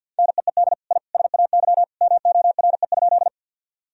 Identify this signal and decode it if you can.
Morse code, heard as a keyed tone, about 47 wpm, 700 Hz. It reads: NEED I SAY MORE?
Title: Sentences from Top 1000 Words - Review Part 1 - 50wpm